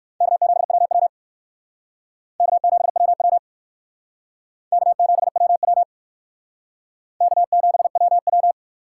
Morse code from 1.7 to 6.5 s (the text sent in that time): K7WW K7WW